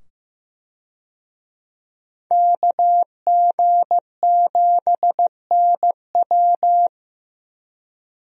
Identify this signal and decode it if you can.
Morse code, heard as a keyed tone, about 15 wpm, 700 Hz. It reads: KG7NW
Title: USA Call Signs - 5 Characters - 15wpm